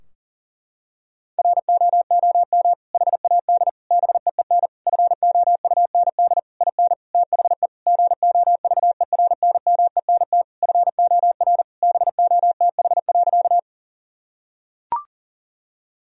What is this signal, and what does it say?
Morse code, heard as a keyed tone, about 40 wpm, 700 Hz. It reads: ROOM HAD BEEN FOUND IN THE GOVERNMENT FOR BOTH.
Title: Sentences from Top 1000 Words - Review Part 3 - 40wpm